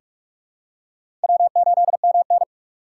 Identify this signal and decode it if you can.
Morse code, heard as a keyed tone, about 46 wpm, 700 Hz. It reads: W8MN
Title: USA Call Signs - 4 Characters - 4x Character Spacing - 45wpm